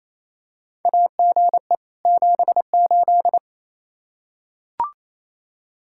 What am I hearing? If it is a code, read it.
Morse code, heard as a keyed tone, about 28 wpm, 700 Hz. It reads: AGE 78
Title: Practice QSO - Exchange 3 - More Info - 28wpm